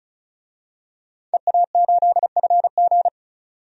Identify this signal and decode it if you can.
Morse code, heard as a keyed tone, about 36 wpm, 700 Hz. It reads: EA8FG
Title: DXCC-100 Call Signs - 5 Characters - 35wpm